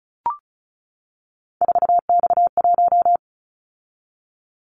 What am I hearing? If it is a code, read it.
Morse code, heard as a keyed tone, about 36 wpm, 690 Hz. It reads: 4X1
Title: Callsign Prefixes for Top 50 DXCC - 35wpm